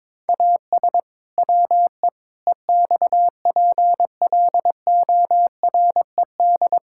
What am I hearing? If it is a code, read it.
Morse code, heard as a keyed tone, about 22 wpm, 700 Hz. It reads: AS WE EXPLORED
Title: Pangram Challenge v2 - 22wpm